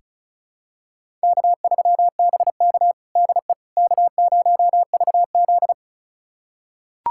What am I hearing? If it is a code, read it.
Morse code, heard as a keyed tone, about 36 wpm, 700 Hz. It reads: K3BK DE K0VZ
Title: USA Callsign DE Callsign - 4 Characters - 35wpm